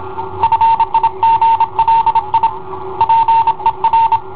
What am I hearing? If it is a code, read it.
Morse code, heard as a keyed tone, about 27 wpm, 910 Hz. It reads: FIGLI PER